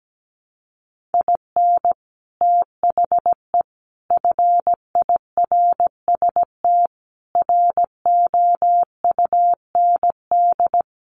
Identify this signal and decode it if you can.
Morse code, heard as a keyed tone, about 17 wpm, 700 Hz. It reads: IN THE FIRST ROUND
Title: Sets of 5 Words Encore 17wpm